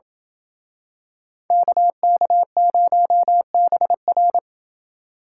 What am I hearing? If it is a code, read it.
Morse code, heard as a keyed tone, about 28 wpm, 700 Hz. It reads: KK0BR